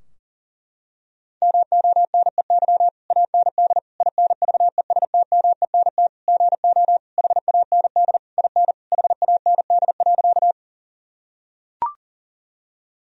Morse code, heard as a keyed tone, about 40 wpm, 700 Hz. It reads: MONEY AND INVESTMENT GO HAND IN HAND.
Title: Sentences from Top 1000 Words - Review Part 4 - 40wpm